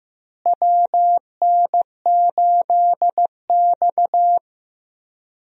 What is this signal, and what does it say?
Morse code, heard as a keyed tone, about 15 wpm, 700 Hz. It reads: WN8X
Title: USA Call Signs - 4 Characters - Repeated 3x - 15wpm